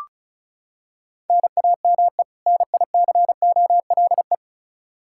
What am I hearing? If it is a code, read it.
Morse code, heard as a keyed tone, about 36 wpm, 700 Hz. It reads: NAME NICOLE
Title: Practice QSO - Exchange 1 - Beginning - 35wpm